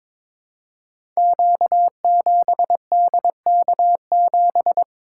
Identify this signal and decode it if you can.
Morse code, heard as a keyed tone, about 22 wpm, 700 Hz. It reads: Q7DK7
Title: Code Groups - 5 Letter:Numbers - 22wpm